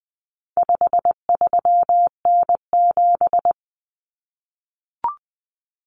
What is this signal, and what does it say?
Morse code, heard as a keyed tone, about 20 wpm, 700 Hz. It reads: 53N7
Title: Common Cut Numbers - 4-Digit 20wpm